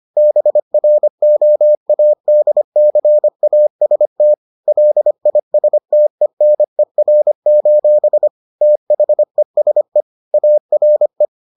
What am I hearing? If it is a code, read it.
Morse code, heard as a keyed tone, about 25 wpm, 600 Hz. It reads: BROADCAST LISTENER: THESE ARE